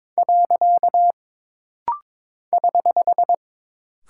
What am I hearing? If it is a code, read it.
Morse code, heard as a keyed tone, about 22 wpm, 700 Hz. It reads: . <HH>
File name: practice / 22wpm / Single Punctuation-Prosign - Rapid-Fire - 22wpm